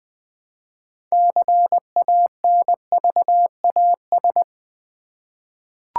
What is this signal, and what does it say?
Morse code, heard as a keyed tone, about 20 wpm, 700 Hz. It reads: CANVAS
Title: 6 Letter Words - First 3 Letters Given - 20wpm